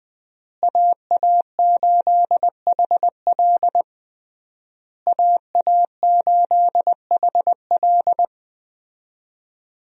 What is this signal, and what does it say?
Morse code, heard as a keyed tone, about 20 wpm, 700 Hz. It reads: AA8HL AA8HL
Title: USA Call Signs - 5 Characters - Repeated 3x - 20wpm